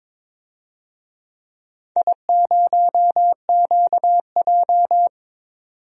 Morse code, heard as a keyed tone, about 22 wpm, 700 Hz. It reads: I0QJ
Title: DXCC-100 Call Signs - 4 Characters - 22wpm